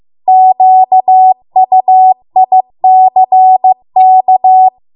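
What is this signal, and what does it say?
Morse code, heard as a keyed tone, about 15 wpm, 750 Hz. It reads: QUICK
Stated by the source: Morse code practice at 15 wpm from the national amateur radio association's code archive.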